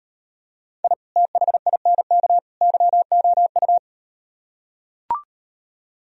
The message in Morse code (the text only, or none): I THINK YOU